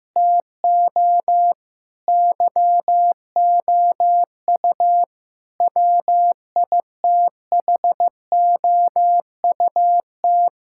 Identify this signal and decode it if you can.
Morse code, heard as a keyed tone, about 15 wpm, 700 Hz. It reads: TO YOU WITHOUT
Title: Sentences from Top 1000 Words - Review Part 6 - 15wpm